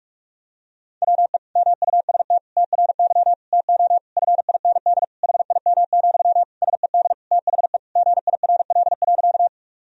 Morse code, heard as a keyed tone, about 41 wpm, 700 Hz. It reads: WE MUST TRY TO FIND HIM, SED THE GIRL.
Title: Sentences from Top 400 Words and Abbreviations 45wpm